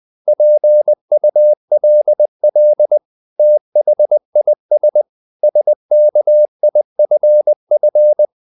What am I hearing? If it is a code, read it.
Morse code, heard as a keyed tone, about 20 wpm, 600 Hz. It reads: PULL THIS SKIFF
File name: the old man and the sea / the old man and the sea part 1 20wpm